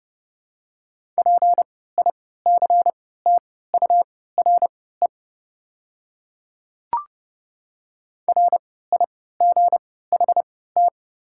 Morse code, heard as a keyed tone, about 30 wpm, spaced as for 17 wpm, 690 Hz. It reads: PICTURE RIGHT
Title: 3x Character Spacing - Top 200 Words - 30wpm